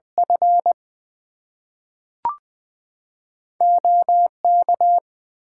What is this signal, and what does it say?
Morse code, heard as a keyed tone, about 20 wpm, 700 Hz. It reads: F OK
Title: Two Letter Words - 20wpm